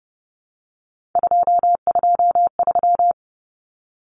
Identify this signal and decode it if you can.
Morse code, heard as a keyed tone, about 30 wpm, 700 Hz. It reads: 223
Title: Common Cut Numbers - 3-Digit 30wpm